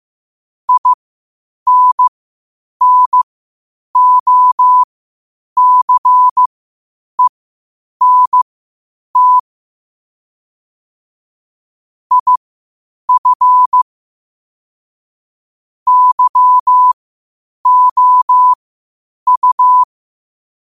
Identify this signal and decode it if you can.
Morse code, heard as a keyed tone, about 15 wpm, spaced as for 8 wpm, 1000 Hz. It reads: INNOCENT IF YOU
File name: qotdfiles / 8wpm